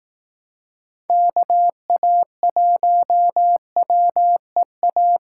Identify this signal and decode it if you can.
Morse code, heard as a keyed tone, about 18 wpm, 700 Hz. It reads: KA1WEA
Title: Speed Racing - USA 2x3 Call Signs 15wpm